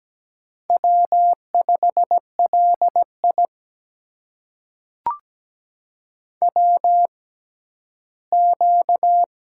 Morse code, heard as a keyed tone, about 17 wpm, spaced as for 12 wpm, 700 Hz. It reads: W5LI WQ
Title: USA Call Signs - 6x Character Spacing - 17wpm